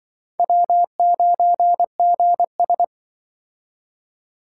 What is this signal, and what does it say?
Morse code, heard as a keyed tone, about 25 wpm, 700 Hz. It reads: W9GS
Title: Speed Racing - USA 1x2 Call Signs 20wpm